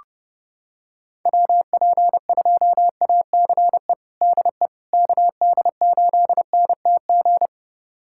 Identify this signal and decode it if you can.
Morse code, heard as a keyed tone, about 30 wpm, 700 Hz. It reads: WP2ACE DE KD8NTG
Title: USA Callsign DE Callsign - 6 Characters - 30wpm